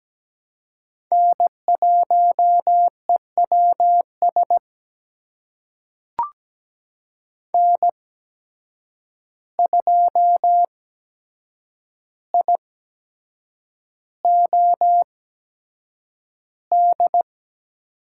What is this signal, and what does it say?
Morse code, heard as a keyed tone, about 17 wpm, spaced as for 7 wpm, 700 Hz. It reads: N1EWS N2IOD
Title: USA Call Signs - 5 Characters - 8x Character Spacing - 17wpm